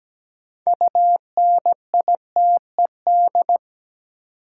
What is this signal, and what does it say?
Morse code, heard as a keyed tone, about 17 wpm, 700 Hz. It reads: UNITED
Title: Sentences from Top 600 Words - New Word Drill - 17wpm